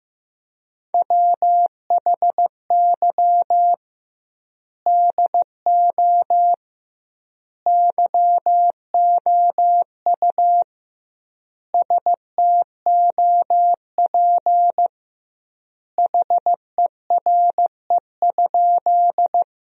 Morse code, heard as a keyed tone, about 16 wpm, 700 Hz. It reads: WHY DO YOU STOP HERE?
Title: 2x Word-Spacing - Sentences from Top 300 Words - 15wpm